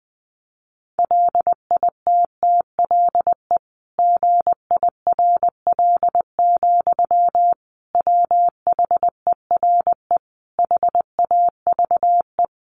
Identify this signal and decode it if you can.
Morse code, heard as a keyed tone, about 20 wpm, 700 Hz. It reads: LITTLE GIRL, WHERE HAVE
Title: Sentences from Top 400 Words 20wpm